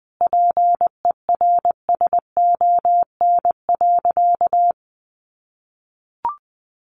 Morse code, heard as a keyed tone, about 20 wpm, 700 Hz. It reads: PERSON.
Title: Sentences from Top 1000 Words 20wpm